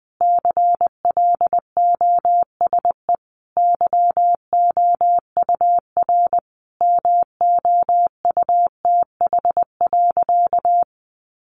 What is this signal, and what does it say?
Morse code, heard as a keyed tone, about 20 wpm, 700 Hz. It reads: CLOSE YOUR MOUTH.
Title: Sentences from Top 1500 Words 20wpm